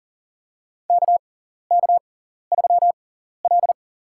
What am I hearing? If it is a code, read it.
Morse code, heard as a keyed tone, about 40 wpm, spaced as for 14 wpm, 700 Hz. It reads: KK3L
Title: USA Call Signs - 6x Character Spacing - 40wpm